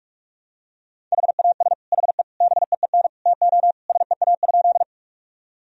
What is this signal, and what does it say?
Morse code, heard as a keyed tone, about 46 wpm, 700 Hz. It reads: HAS HE BEEN TO SEA?
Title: Sentences from Top 1000 Words - Review Part 1 - 45wpm